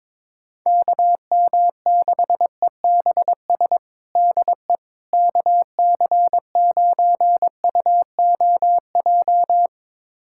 Morse code, heard as a keyed tone, about 22 wpm, 700 Hz. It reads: KM6EBS DE KC9UOJ
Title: USA Callsign DE Callsign - 6 Characters - 22wpm